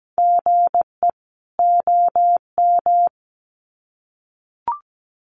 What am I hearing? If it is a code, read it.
Morse code, heard as a keyed tone, about 17 wpm, 700 Hz. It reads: GE OM
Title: Common QSO Phrases 17wpm